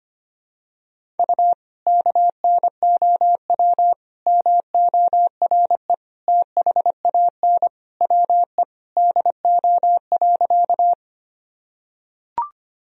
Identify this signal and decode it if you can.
Morse code, heard as a keyed tone, about 25 wpm, 700 Hz. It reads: U KNOW MORE THAN WE DO.
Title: Sentences from Top 100 Words and Abbreviations 25wpm